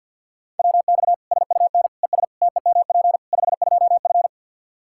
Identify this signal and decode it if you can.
Morse code, heard as a keyed tone, about 52 wpm, 700 Hz. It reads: WX SUN ES TEMP 52F
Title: Practice QSO - Exchange 2 - Info - 50wpm